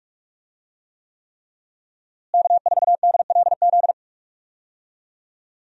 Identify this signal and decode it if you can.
Morse code, heard as a keyed tone, about 44 wpm, 700 Hz. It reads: K4DLZ
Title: USA Call Signs - 5 Characters - Repeated 3x - 45wpm